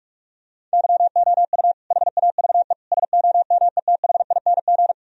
Morse code, heard as a keyed tone, about 46 wpm, 700 Hz. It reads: YOU HAVE SOMETHING
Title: Sentences from Top 1000 Words - Review Part 4 - 45wpm